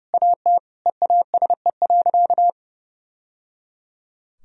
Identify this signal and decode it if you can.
Morse code, heard as a keyed tone, about 30 wpm, 700 Hz. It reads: AT EASE.
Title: Sentences from Top 1000 Words - Review Part 1 - 30wpm